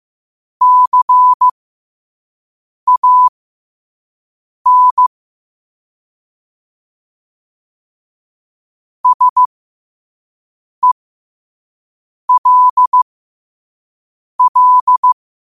Morse code, heard as a keyed tone, about 15 wpm, spaced as for 5 wpm, 1000 Hz. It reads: CAN SELL
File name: 5wpm